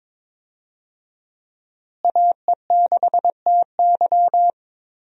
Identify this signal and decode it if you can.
Morse code, heard as a keyed tone, about 22 wpm, 700 Hz. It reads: AE6TY
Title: USA Call Signs - 5 Characters - Repeated 2x - 22wpm